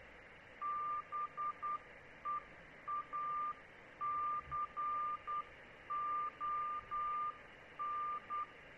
Morse code, heard as a keyed tone, about 10 wpm, 1190 Hz, with noise. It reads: BEACON